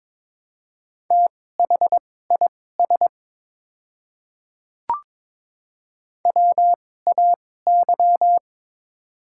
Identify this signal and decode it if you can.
Morse code, heard as a keyed tone, about 21 wpm, spaced as for 16 wpm, 700 Hz. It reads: THIS WAY
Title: 2x Character Spacing - Top 200 Words - 22wpm